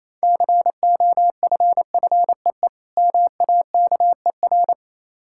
Morse code, heard as a keyed tone, about 27 wpm, 700 Hz. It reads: COFFEE MAKER
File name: Easy Words - Speed-Racing with Hints - 22wpm